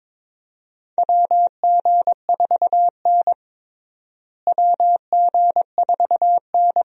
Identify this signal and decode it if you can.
Morse code, heard as a keyed tone, about 22 wpm, 700 Hz. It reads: WG4N WG4N
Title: USA Call Signs - 4 Characters - Repeated 3x - 22wpm